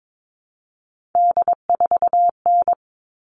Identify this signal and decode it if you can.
Morse code, heard as a keyed tone, about 22 wpm, 700 Hz. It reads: D4N
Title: All Cut Numbers - 3-Digit 22wpm